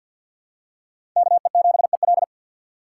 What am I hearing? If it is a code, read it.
Morse code, heard as a keyed tone, about 47 wpm, 690 Hz. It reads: KE7EL
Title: USA Call Signs - 5 Characters - 3x Character Spacing - 50wpm